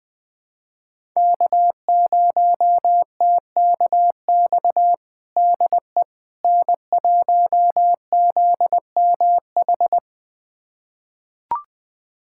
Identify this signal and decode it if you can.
Morse code, heard as a keyed tone, about 21 wpm, 700 Hz. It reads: K0TKX DE N1ZMH
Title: USA Callsign DE Callsign - 5 Characters - 20wpm